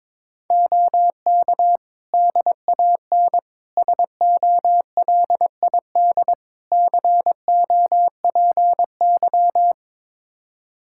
Morse code, heard as a keyed tone, about 22 wpm, 700 Hz. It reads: OK DAN SOLID COPY
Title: Practice QSO - Exchange 3 - More Info - 22wpm